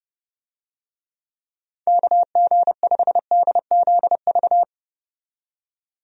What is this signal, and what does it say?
Morse code, heard as a keyed tone, about 30 wpm, 700 Hz. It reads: KG5DZV
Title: USA Call Signs - 6 Characters - 30wpm